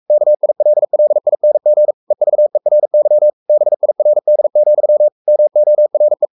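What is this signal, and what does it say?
Morse code, heard as a keyed tone, about 44 wpm, 600 Hz. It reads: KILLING EVERY BIRD, MORE
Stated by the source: Morse code at 43 wpm sending Ken Follett's novel Never in English, from CW med Gnister.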